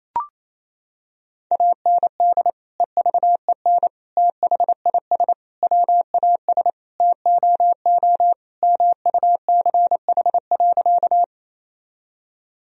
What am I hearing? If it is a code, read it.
Morse code, heard as a keyed tone, about 28 wpm, 700 Hz. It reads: AND EVEN THIS WAS TOO MUCH.
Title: Sentences from Top 1000 Words - Review Part 1 - 28wpm